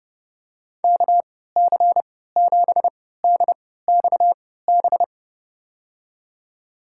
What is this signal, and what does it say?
Morse code, heard as a keyed tone, about 32 wpm, spaced as for 17 wpm, 700 Hz. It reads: KC7DXB
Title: USA Call Signs - 6 Characters - 3x Character Spacing - 30wpm